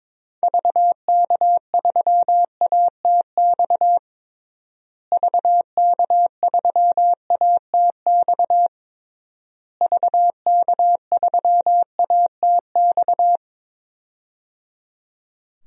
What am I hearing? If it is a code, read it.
Morse code, heard as a keyed tone, about 22 wpm, 700 Hz. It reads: VK3ATX VK3ATX VK3ATX
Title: DXCC-100 Call Signs - 6 Characters - Repeated 3x - 22wpm